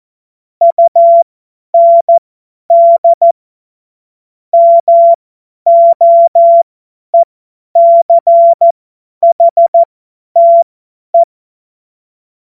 Morse code, heard as a keyed tone, about 14 wpm, spaced as for 10 wpm, 680 Hz. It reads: UND MOECHTE
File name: Morsekurs / Witze / Witz 09 April 2024 50 F70